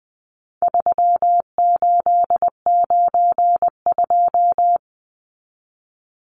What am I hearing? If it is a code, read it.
Morse code, heard as a keyed tone, about 20 wpm, 700 Hz. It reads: 3892
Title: Numbers - 4-Digit 20wpm